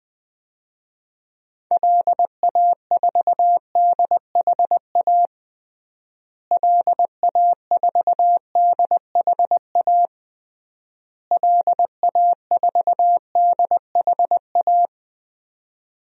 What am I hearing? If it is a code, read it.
Morse code, heard as a keyed tone, about 20 wpm, 700 Hz. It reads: LA4DHA LA4DHA LA4DHA
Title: DXCC-100 Call Signs - 6 Characters - Repeated 3x - 20wpm